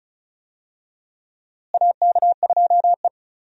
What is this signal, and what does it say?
Morse code, heard as a keyed tone, about 36 wpm, 700 Hz. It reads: AK2E
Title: USA Call Signs - 4 Characters - Repeated 2x - 35wpm